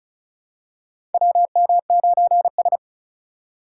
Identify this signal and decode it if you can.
Morse code, heard as a keyed tone, about 34 wpm, 700 Hz. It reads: WM9S
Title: USA Call Signs - 4 Characters - 4x Character Spacing - 35wpm